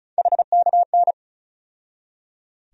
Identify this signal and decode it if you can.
Morse code, heard as a keyed tone, about 34 wpm, 700 Hz. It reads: HKN
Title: USA Call Sign Postfixes - 3 Letters 35wpm